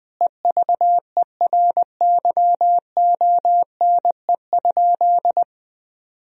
Morse code, heard as a keyed tone, about 21 wpm, 700 Hz. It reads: EVERYONE?
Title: Sentences from Top 1000 Words - Review Part 4 - 20wpm